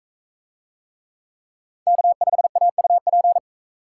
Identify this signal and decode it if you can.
Morse code, heard as a keyed tone, about 39 wpm, 700 Hz. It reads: K5AUP